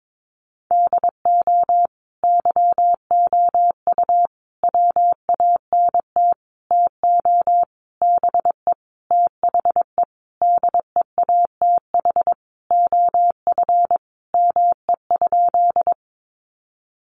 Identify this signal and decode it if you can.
Morse code, heard as a keyed tone, about 22 wpm, 700 Hz. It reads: DO YOU WANT TO BE THE DEATH OF ME?
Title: Sentences from Top 1100 Words 22wpm